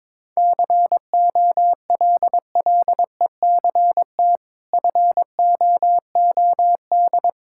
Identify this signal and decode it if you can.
Morse code, heard as a keyed tone, about 22 wpm, 700 Hz. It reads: COLLECT FOOD
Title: Sentences from Top 1600 Words 22wpm